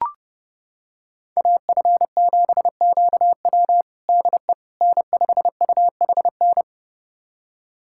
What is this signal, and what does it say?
Morse code, heard as a keyed tone, about 32 wpm, 700 Hz. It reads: AF7QW DE N5UHN
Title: USA Callsign DE Callsign - 5 Characters - 30wpm